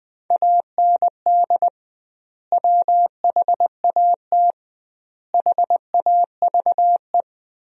Morse code, heard as a keyed tone, about 20 wpm, 700 Hz. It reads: AND WHAT HAVE
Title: 2x Word-Spacing - Sentences from Top 500 Words - 20wpm